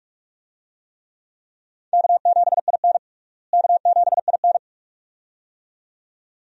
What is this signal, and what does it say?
Morse code, heard as a keyed tone, about 46 wpm, 700 Hz. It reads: K7IN K7IN
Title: USA Call Signs - 4 Characters - Repeated 2x - 45wpm